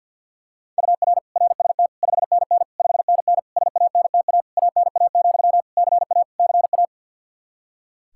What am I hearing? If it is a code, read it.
Morse code, heard as a keyed tone, about 48 wpm, 700 Hz. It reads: UR RST 5NN 5NN SANTA ANA, CA CA
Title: Practice QSO - Exchange 1 - Beginning - 50wpm